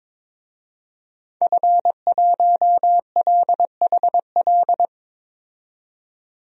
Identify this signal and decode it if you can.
Morse code, heard as a keyed tone, about 22 wpm, 700 Hz. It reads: F1LHL
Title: DXCC-100 Call Signs - 5 Characters - 22wpm